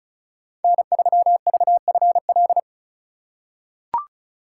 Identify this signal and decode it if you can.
Morse code, heard as a keyed tone, about 34 wpm, 700 Hz. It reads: N3VFL
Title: USA Call Signs - 5 Characters - 3x Character Spacing - 35wpm